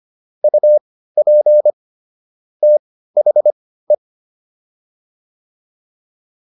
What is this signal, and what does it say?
Morse code, heard as a keyed tone, about 25 wpm, spaced as for 15 wpm, 600 Hz. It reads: UP THE